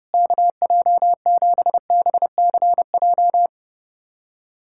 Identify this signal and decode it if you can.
Morse code, heard as a keyed tone, about 30 wpm, 700 Hz. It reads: KJ7BCJ
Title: USA Callsign DE Callsign 30wpm